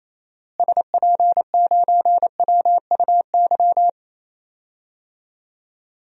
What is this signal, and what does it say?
Morse code, heard as a keyed tone, about 28 wpm, 700 Hz. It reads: SP9WUY
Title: DXCC-100 Call Signs - 6 Characters - Repeated 2x - 28wpm